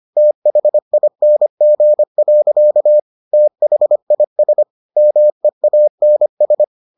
Morse code, heard as a keyed tone, about 25 wpm, 600 Hz. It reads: THING. THIS MEANS